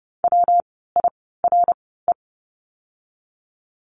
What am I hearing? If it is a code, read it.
Morse code, heard as a keyed tone, about 30 wpm, 700 Hz. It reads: W I R E
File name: course / 30:10wpm / 115 Lesson 19f - Letter w - Words 30wpm